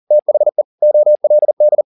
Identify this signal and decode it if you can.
Morse code, heard as a keyed tone, about 40 wpm, 600 Hz. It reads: THE OLD